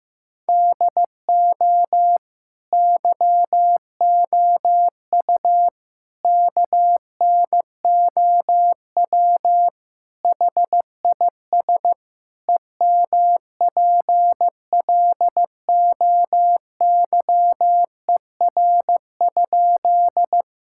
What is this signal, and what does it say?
Morse code, heard as a keyed tone, about 15 wpm, 700 Hz. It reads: DO YOU KNOW HIS EMPLOYER?
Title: Sentences from Top 1600 Words 15wpm